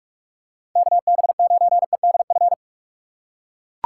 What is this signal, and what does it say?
Morse code, heard as a keyed tone, about 44 wpm, 700 Hz. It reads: KB9EDF
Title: USA Call Signs - 6 Characters - 6x Character Spacing - 45wpm